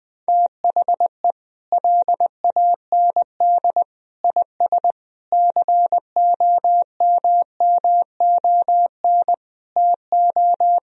Morse code, heard as a keyed tone, about 20 wpm, 700 Hz. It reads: THE LAND IS COMMON TO
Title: Sentences from Top 1000 Words - Review Part 3 - 20wpm